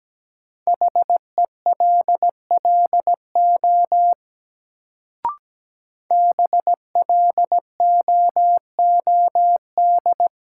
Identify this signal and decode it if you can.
Morse code, heard as a keyed tone, about 17 wpm, 700 Hz. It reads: HELLO BLOOD
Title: Sentences from Top 1100 Words - New Word Drill Alt - 17wpm